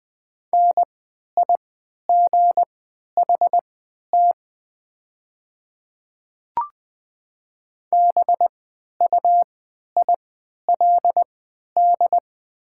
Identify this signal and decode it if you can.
Morse code, heard as a keyed tone, about 20 wpm, spaced as for 11 wpm, 690 Hz. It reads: NIGHT BUILD
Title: 3x Character Spacing - Top 500 Words - 20wpm